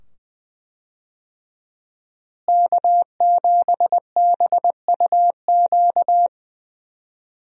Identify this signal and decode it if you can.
Morse code, heard as a keyed tone, about 21 wpm, 700 Hz. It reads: K7BUQ
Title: USA Call Signs - 5 Characters - Repeated 2x - 20wpm